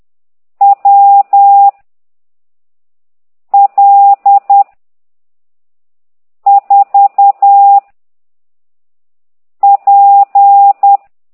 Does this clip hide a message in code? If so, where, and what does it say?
Yes: Morse code from 0.6 to 11.0 s, heard as a keyed tone, about 10 wpm, 800 Hz. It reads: W L 4 P